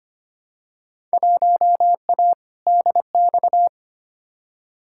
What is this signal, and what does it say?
Morse code, heard as a keyed tone, about 25 wpm, 700 Hz. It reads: 1A DX
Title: Field Day Exchanges Encore - 25wpm